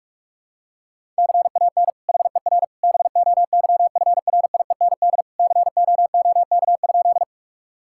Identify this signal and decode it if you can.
Morse code, heard as a keyed tone, about 46 wpm, 700 Hz. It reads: CAN HER BOYFRIEND COOK?
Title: Sentences from Top 1000 Words - Review Part 5 - 45wpm